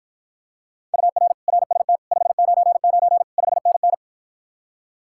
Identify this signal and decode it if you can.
Morse code, heard as a keyed tone, about 53 wpm, 700 Hz. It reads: UR RST 599 5NN